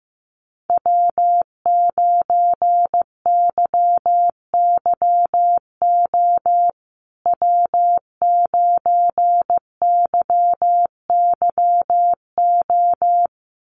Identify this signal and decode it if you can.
Morse code, heard as a keyed tone, about 15 wpm, 700 Hz. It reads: W9YYO W9YYO
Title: USA Call Signs Repeated 2x 15wpm